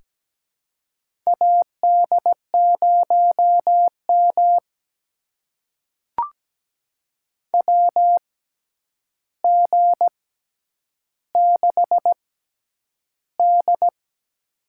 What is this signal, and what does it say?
Morse code, heard as a keyed tone, about 17 wpm, spaced as for 9 wpm, 700 Hz. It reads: AD0M WG6D
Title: USA Call Signs - 4 Characters - 6x Character Spacing - 17wpm